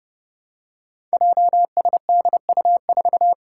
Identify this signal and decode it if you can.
Morse code, heard as a keyed tone, about 28 wpm, 700 Hz. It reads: JSDU4
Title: Five Challenging Letter-Number Combinations 30wpm